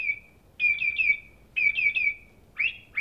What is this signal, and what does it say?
There is no code.